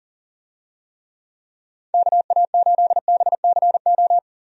Morse code, heard as a keyed tone, about 40 wpm, 700 Hz. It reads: KA8BCO
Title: USA Call Signs - 6 Characters - 40wpm